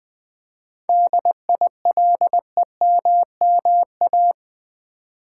Morse code, heard as a keyed tone, about 20 wpm, 700 Hz. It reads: DILEMMA
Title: 7 Letter Words - First 3 Letters Given - 20wpm